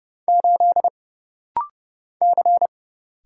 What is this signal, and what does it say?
Morse code, heard as a keyed tone, about 30 wpm, 700 Hz. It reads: 8 C
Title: Single Letter-Number - Rapid-Fire 30wpm